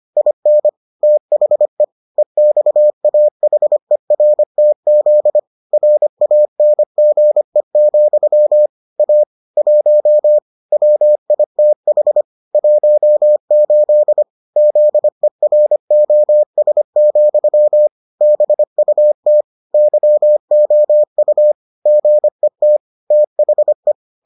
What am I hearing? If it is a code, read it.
Morse code, heard as a keyed tone, about 25 wpm, 600 Hz. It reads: IN THE EXAHERTZ RANGE, A 1 WITH 18 ZEROS, BUT YOU GET THE